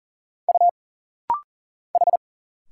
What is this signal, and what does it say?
Morse code, heard as a keyed tone, about 40 wpm, 700 Hz. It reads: U H
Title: Single Letter-Number Alt - Rapid-Fire 40wpm